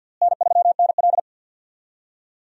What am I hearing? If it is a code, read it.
Morse code, heard as a keyed tone, about 52 wpm, 700 Hz. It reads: N3NL
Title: USA Call Signs - 4 Characters - Repeated 2x - 50wpm